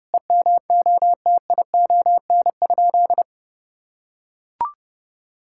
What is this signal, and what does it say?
Morse code, heard as a keyed tone, about 29 wpm, 700 Hz. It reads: EMOTION?
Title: Sentences from Top 1600 Words 30wpm